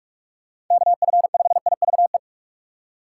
Morse code, heard as a keyed tone, about 46 wpm, 700 Hz. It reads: KF5IVE
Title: USA Call Signs - 6 Characters - 4x Character Spacing - 45wpm